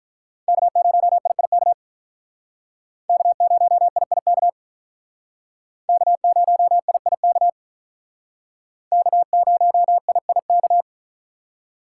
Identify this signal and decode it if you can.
Morse code, heard as a keyed tone, about 44 wpm, 700 Hz. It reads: K0IIK K0IIK K0IIK K0IIK
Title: Speed Racing - USA 1x3 Call Signs 35wpm